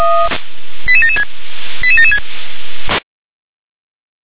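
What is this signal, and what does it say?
Five-tone selective call (ZVEI), heard as quick runs of five tones, tones of about 70 ms. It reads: 88776 88776
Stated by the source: Five-tone ZVEI selective call received by SDR.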